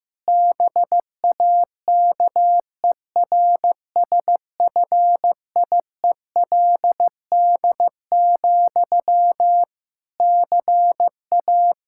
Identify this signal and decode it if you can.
Morse code, heard as a keyed tone, about 16 wpm, 700 Hz. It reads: BAKERSFIELD, CA
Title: Practice QSO - Exchange 1 - Beginning - 15wpm